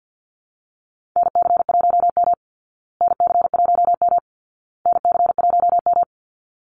Morse code, heard as a keyed tone, about 50 wpm, 700 Hz. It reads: NC1M NC1M NC1M
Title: Call Signs Repeated 2x - 50wpm